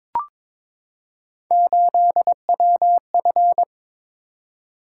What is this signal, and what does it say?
Morse code, heard as a keyed tone, about 22 wpm, 700 Hz. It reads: 8WF
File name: practice / 22wpm / Three Letter-Number Combinations - 22wpm